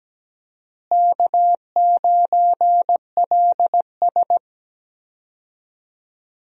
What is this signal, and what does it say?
Morse code, heard as a keyed tone, about 17 wpm, 700 Hz. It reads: K9LS
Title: USA Call Signs - 4 Characters - Repeated 2x - 17wpm